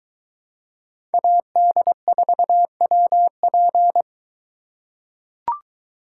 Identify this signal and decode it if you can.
Morse code, heard as a keyed tone, about 21 wpm, 700 Hz. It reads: AD4WP